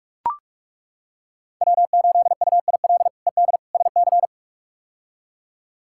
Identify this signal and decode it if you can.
Morse code, heard as a keyed tone, about 41 wpm, 700 Hz. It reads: W8UIL ED SC
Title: K1USN SST QSO Practice - 45wpm